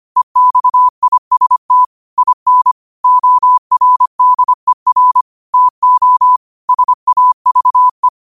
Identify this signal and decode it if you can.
Morse code, heard as a keyed tone, about 25 wpm, 1000 Hz. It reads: EXIST IN ORDER TO SAVE
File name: qotdfiles / 25wpm